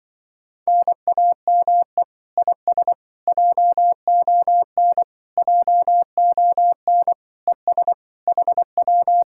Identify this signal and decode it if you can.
Morse code, heard as a keyed tone, about 24 wpm, 700 Hz. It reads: NAME IS JON JON ES HW
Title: Speed-Racing Common QSO Phrases - 20wpm